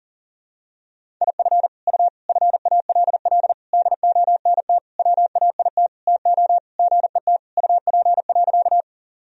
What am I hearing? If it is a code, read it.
Morse code, heard as a keyed tone, about 40 wpm, 700 Hz. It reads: IF U FALL DONT WAIT TO GET UP.
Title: Sentences from Top 500 Words and Abbreviations 40wpm